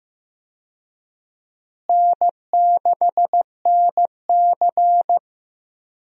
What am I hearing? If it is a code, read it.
Morse code, heard as a keyed tone, about 15 wpm, 700 Hz. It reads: N6NC
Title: USA Call Signs - 4 Characters - Repeated 3x - 15wpm